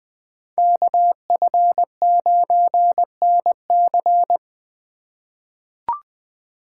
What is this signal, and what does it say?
Morse code, heard as a keyed tone, about 20 wpm, 700 Hz. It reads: KF9NC
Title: USA Call Signs - 5 Characters - 3x Character Spacing - 20wpm